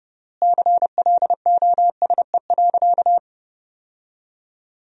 Morse code, heard as a keyed tone, about 29 wpm, 700 Hz. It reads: CLOSE.
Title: 3x Word-Spacing - Sentences from Top 300 Words - 30wpm